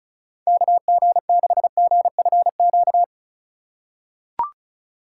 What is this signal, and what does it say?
Morse code, heard as a keyed tone, about 34 wpm, 700 Hz. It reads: KG6GFQ
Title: USA Call Signs - 6 Characters - 3x Character Spacing - 35wpm